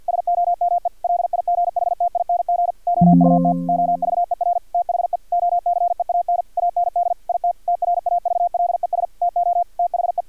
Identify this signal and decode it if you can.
Morse code, heard as a keyed tone, about 50 wpm, 700 Hz. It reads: SOME LIGHTING OUT OVER THE OCEAN AND IT TRAVLES TO THE
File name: archived / CRSnet